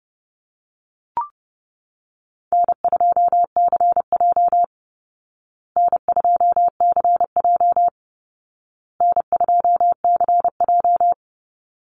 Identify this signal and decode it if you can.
Morse code, heard as a keyed tone, about 28 wpm, 700 Hz. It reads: N2CJ N2CJ N2CJ
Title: Easy Call Signs Repeated 3x - 30wpm